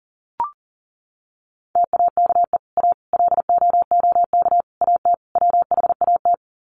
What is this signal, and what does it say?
Morse code, heard as a keyed tone, about 40 wpm, 700 Hz. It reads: TAKE A LOOK AT WHAT
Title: Sets of 5 Words Encore2 40wpm